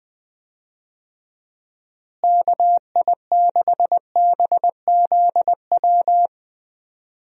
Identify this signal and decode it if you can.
Morse code, heard as a keyed tone, about 21 wpm, 700 Hz. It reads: KI6BZW
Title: USA Call Signs - 6 Characters - Repeated 3x - 20wpm